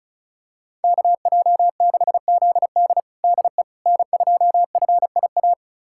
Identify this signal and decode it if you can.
Morse code, heard as a keyed tone, about 36 wpm, 700 Hz. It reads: KJ6ZD DE N2FIA
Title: USA Callsign DE Callsign - 5 Characters - 35wpm